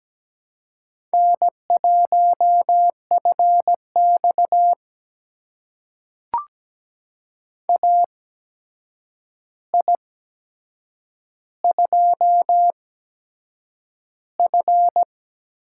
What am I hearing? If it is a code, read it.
Morse code, heard as a keyed tone, about 17 wpm, spaced as for 7 wpm, 700 Hz. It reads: N1FX AI2F
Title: USA Call Signs - 4 Characters - 8x Character Spacing - 17wpm